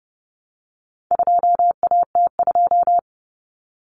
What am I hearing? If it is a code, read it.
Morse code, heard as a keyed tone, about 30 wpm, 700 Hz. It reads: 2AT2
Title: Common Cut Numbers - 4-Digit 30wpm